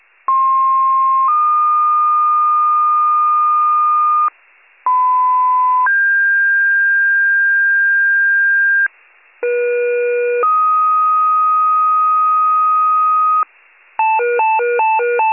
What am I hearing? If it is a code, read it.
Two-tone page, heard as a one-second tone, then a three-second tone, about 1060 then 1220 Hz, about 1010 then 1600 Hz, about 480 then 1250 Hz. It is heard from 0.3 to 13.4 s.
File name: pager / 23 FIRE / wcalert